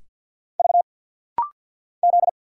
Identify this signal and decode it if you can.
Morse code, heard as a keyed tone, about 46 wpm, 700 Hz. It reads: V Z
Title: Single Letters - Rapid-Fire 50wpm